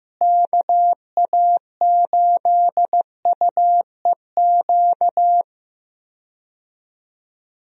Morse code, heard as a keyed tone, about 15 wpm, 700 Hz. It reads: KA8UEQ
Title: USA Call Signs - 6 Characters - Repeated 3x - 15wpm